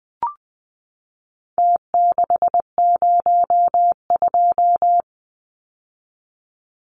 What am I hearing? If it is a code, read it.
Morse code, heard as a keyed tone, about 20 wpm, 700 Hz. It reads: T602
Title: Numbers - Contest Style - 4-Digit 20wpm